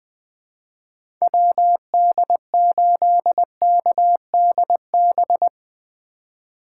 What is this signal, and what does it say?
Morse code, heard as a keyed tone, about 21 wpm, 700 Hz. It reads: WD8KDB
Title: 2023 SOTA Chaser Call Signs - 20wpm